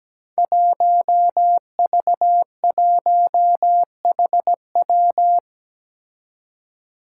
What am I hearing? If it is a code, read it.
Morse code, heard as a keyed tone, about 17 wpm, 700 Hz. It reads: 1V1HW
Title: Five Challenging Letter-Number Combinations 17wpm